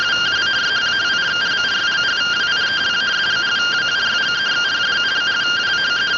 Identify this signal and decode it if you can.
Two-tone teleprinter signal, about 170 Hz shift.